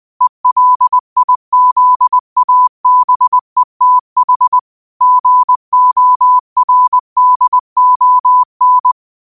Morse code, heard as a keyed tone, about 20 wpm, 1000 Hz. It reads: ELIZABETH GORDON